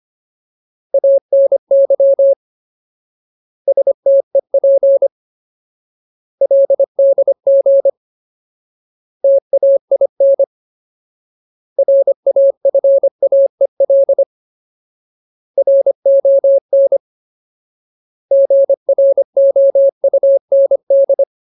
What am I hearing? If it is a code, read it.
Morse code, heard as a keyed tone, about 25 wpm, 550 Hz. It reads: ANY STEP LDG TAIN RAFAEL RON GROUND
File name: Ultimate Word List / 25WPM Ultimate